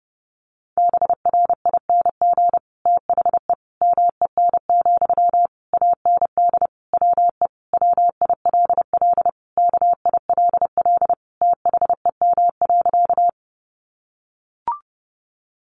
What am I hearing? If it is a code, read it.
Morse code, heard as a keyed tone, about 30 wpm, 700 Hz. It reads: BRING THE MEN, AND WE WILL KILL THEM.
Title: Sentences from Top 1400 Words 30wpm